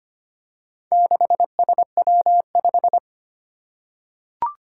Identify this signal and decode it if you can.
Morse code, heard as a keyed tone, about 25 wpm, 700 Hz. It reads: 6SW5
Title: Four Challenging Letter-Number Combinations 25wpm